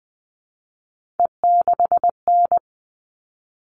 Morse code, heard as a keyed tone, about 20 wpm, 700 Hz. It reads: E6N
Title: All Cut Numbers - 3-Digit 20wpm